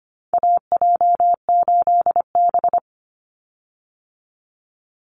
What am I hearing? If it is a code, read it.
Morse code, heard as a keyed tone, about 25 wpm, 700 Hz. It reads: AJ8B
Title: Call Signs Repeated 2x - 25wpm